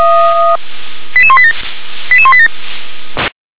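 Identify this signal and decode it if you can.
Five-tone selective call (ZVEI), heard as quick runs of five tones, tones of about 70 ms. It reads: 88187 88187
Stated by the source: ZVEI five-tone selective call received with an SDR.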